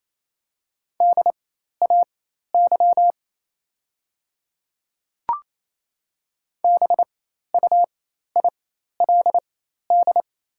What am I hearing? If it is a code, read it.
Morse code, heard as a keyed tone, about 27 wpm, spaced as for 13 wpm, 700 Hz. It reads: DAY BUILD
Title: 4x Character Spacing - Top 200 Words - 28wpm